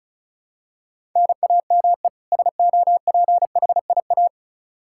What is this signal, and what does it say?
Morse code, heard as a keyed tone, about 36 wpm, 700 Hz. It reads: NAME SOPHIA
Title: Practice QSO - Exchange 1 - Beginning - 35wpm